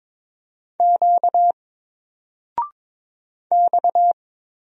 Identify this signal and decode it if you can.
Morse code, heard as a keyed tone, about 22 wpm, 700 Hz. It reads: Q X
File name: Single-Letter-Number-22wpm